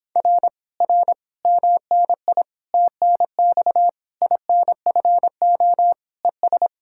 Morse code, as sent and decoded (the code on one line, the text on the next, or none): .-. / .-. / -- -. .. / - -. -..- / .. -. ..-. --- / . ...
R R MNI TNX INFO ES